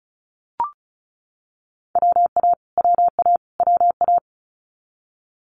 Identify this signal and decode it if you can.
Morse code, heard as a keyed tone, about 36 wpm, 700 Hz. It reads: WA WA WA
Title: Multi-letter Phonemes Repeated 3x 35wpm